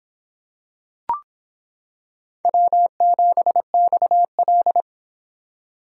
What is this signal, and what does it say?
Morse code, heard as a keyed tone, about 26 wpm, 700 Hz. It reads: W7XL